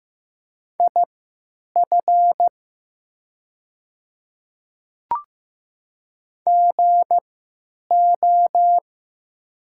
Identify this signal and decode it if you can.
Morse code, heard as a keyed tone, about 15 wpm, spaced as for 9 wpm, 700 Hz. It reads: IF GO
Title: 3x Character Spacing - Top 100 Words - 15wpm